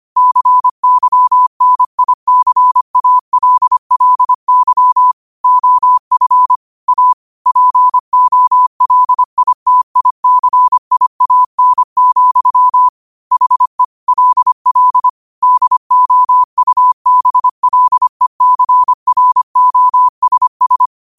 Morse code, heard as a keyed tone, about 25 wpm, 1000 Hz. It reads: CYNICALLY OF A POLITICIAN, HE LL DOUBLECROSS
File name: qotdfiles / 25wpm